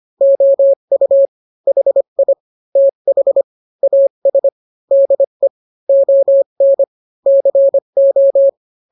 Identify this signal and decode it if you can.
Morse code, heard as a keyed tone, about 25 wpm, 550 Hz. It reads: OU HI TH AS DE ON CO